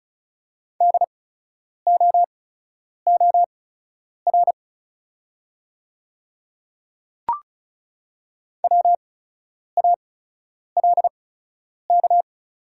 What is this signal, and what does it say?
Morse code, heard as a keyed tone, about 36 wpm, spaced as for 10 wpm, 690 Hz. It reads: DOOR WALK
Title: 8x Character Spacing - Top 300 Words - 35wpm